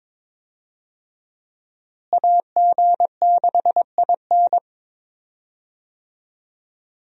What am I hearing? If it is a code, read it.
Morse code, heard as a keyed tone, about 22 wpm, 700 Hz. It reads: AG6IN
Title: USA Call Signs - 5 Characters - Repeated 3x - 22wpm